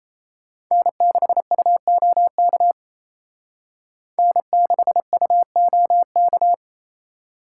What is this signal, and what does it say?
Morse code, heard as a keyed tone, about 34 wpm, 700 Hz. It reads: N6UOK N6UOK
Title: Speed Racing - USA 1x3 Call Signs 28wpm